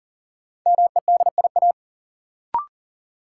Morse code, heard as a keyed tone, about 40 wpm, 700 Hz. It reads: MEDIA
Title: Sentences from Top 1000 Words - Review Part 4 - New Word Drill - 40wpm